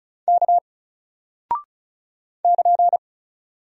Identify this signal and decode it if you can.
Morse code, heard as a keyed tone, about 36 wpm, 700 Hz. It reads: K <KN>
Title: Single Character (Letter:Number:Punctuation:Prosign) Alt 35wpm